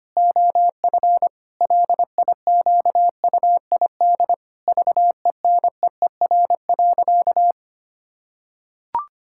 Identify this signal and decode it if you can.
Morse code, heard as a keyed tone, about 25 wpm, 700 Hz. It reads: OF LIQUID VENEER.
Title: Pangram Challenge v2 - 25wpm